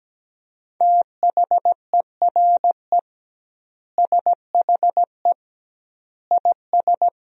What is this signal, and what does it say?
Morse code, heard as a keyed tone, about 17 wpm, 700 Hz. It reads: THERE SHE IS
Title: 2x Word-Spacing - Sentences from Top 200 Words - 17wpm